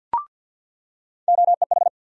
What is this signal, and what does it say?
Morse code, heard as a keyed tone, about 50 wpm, 700 Hz. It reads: OEH
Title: USA Call Sign Postfixes - 3 Letters 50wpm